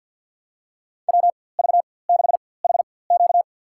Morse code, heard as a keyed tone, about 52 wpm, spaced as for 23 wpm, 700 Hz. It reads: WV6HQ
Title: USA Call Signs - 5 Characters - 4x Character Spacing - 50wpm